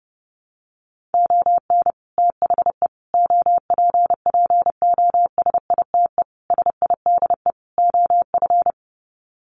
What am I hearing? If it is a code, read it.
Morse code, heard as a keyed tone, about 30 wpm, 700 Hz. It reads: ON THE OPPOSITE SIDE OF
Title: Sets of 5 Words Encore2 30wpm